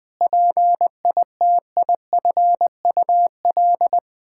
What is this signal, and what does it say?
Morse code, heard as a keyed tone, about 20 wpm, 700 Hz. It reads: PITIFUL
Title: Pangram Challenge v2 - 20wpm